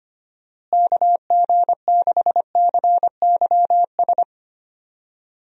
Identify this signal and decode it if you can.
Morse code, heard as a keyed tone, about 25 wpm, 700 Hz. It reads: KG6CYS
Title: USA Call Signs - 6 Characters - Repeated 2x - 25wpm